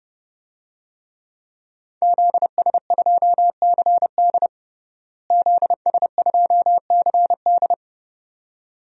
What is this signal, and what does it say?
Morse code, heard as a keyed tone, about 30 wpm, 700 Hz. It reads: ZS2CD ZS2CD
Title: DXCC-100 Call Signs - 5 Characters - Repeated 2x - 30wpm